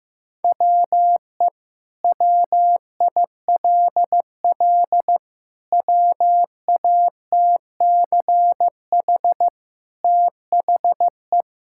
Morse code, heard as a keyed tone, about 15 wpm, 700 Hz. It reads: WE WILL WATCH THE
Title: Sentences from Top 1000 Words - Review Part 5 - 15wpm